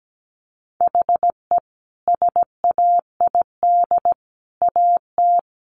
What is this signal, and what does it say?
Morse code, heard as a keyed tone, about 17 wpm, 700 Hz. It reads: HE SAID AT
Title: Sentences from Top 300 Words 17wpm